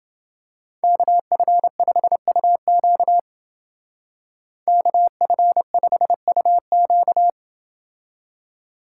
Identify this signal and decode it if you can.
Morse code, heard as a keyed tone, about 28 wpm, 700 Hz. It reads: KF5UQ KF5UQ